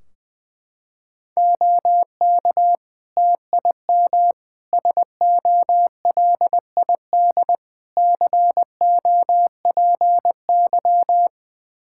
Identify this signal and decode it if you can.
Morse code, heard as a keyed tone, about 21 wpm, 700 Hz. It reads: OK TIM SOLID COPY
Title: Practice QSO - Exchange 3 - More Info - 20wpm